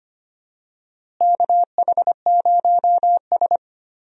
Morse code, heard as a keyed tone, about 25 wpm, 700 Hz. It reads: KH0S
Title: USA Call Signs - 4 Characters - Repeated 3x - 25wpm